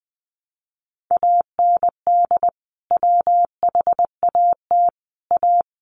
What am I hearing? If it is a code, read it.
Morse code, heard as a keyed tone, about 20 wpm, 700 Hz. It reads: AND WHAT A
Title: Sentences from Top 600 Words 20wpm